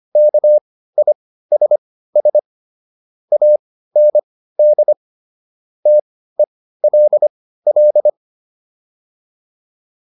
Morse code, heard as a keyed tone, about 25 wpm, spaced as for 15 wpm, 600 Hz. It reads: KISS AND TELL